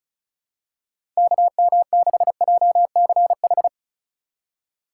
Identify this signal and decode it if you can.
Morse code, heard as a keyed tone, about 34 wpm, 700 Hz. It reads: KM6JCH
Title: USA Call Signs - 2x Character Spacing - 35wpm